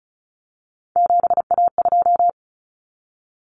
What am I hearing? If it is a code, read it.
Morse code, heard as a keyed tone, about 34 wpm, 700 Hz. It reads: 7A2
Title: Common Cut Numbers - 3-Digit 35wpm